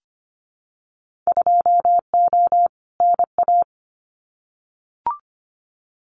Morse code, heard as a keyed tone, about 25 wpm, 700 Hz. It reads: 2O NA
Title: Winter Field Day Exchanges 25wpm